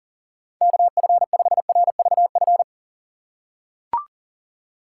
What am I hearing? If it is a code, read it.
Morse code, heard as a keyed tone, about 37 wpm, 700 Hz. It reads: KF5RVF
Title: USA Call Signs - 6 Characters - 4x Character Spacing - 40wpm